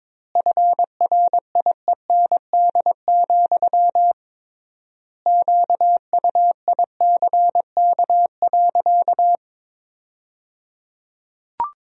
Morse code, heard as a keyed tone, about 21 wpm, 700 Hz. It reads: FRIEND, QUICK.
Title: 3x Word-Spacing - Sentences from Top 500 Words - 22wpm